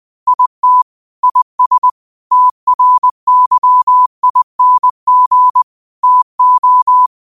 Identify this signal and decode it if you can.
Morse code, heard as a keyed tone, about 20 wpm, 1000 Hz. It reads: IT IS TRYING TO